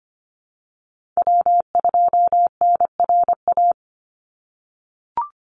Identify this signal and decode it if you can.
Morse code, heard as a keyed tone, about 25 wpm, 700 Hz. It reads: W2NRA
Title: Call Signs - 25wpm